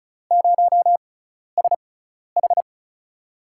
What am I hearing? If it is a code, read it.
Morse code, heard as a keyed tone, about 36 wpm, spaced as for 12 wpm, 700 Hz. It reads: 0SH
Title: USA Call Signs - 4 Characters - 6x Character Spacing - 35wpm